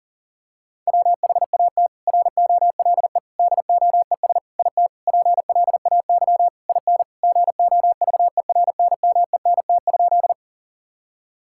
Morse code, heard as a keyed tone, about 40 wpm, 700 Hz. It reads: WHAT ROLE DOES IT PLAY IN GOVERNMENT?
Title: Sentences from Top 1000 Words - Review Part 6 - 40wpm